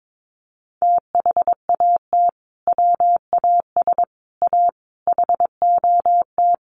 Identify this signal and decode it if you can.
Morse code, heard as a keyed tone, about 22 wpm, 700 Hz. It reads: THAT WAS A HOT
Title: Sentences from Top 1100 Words 22wpm